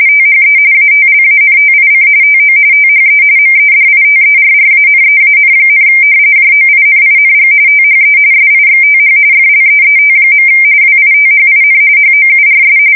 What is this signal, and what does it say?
RTTY, about 45.45 baud. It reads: THE ATLANTIC TO THE MEDITERRANEAN, JOINING THE SIXTH FLEET. WITHIN HOURS
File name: fsk170 / battleshipnj